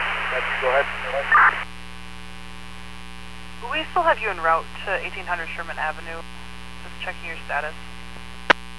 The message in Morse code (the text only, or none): none